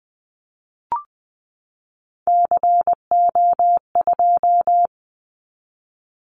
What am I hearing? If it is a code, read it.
Morse code, heard as a keyed tone, about 20 wpm, 700 Hz. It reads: CO2
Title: Callsign Prefixes for Top 100 DXCC - 20wpm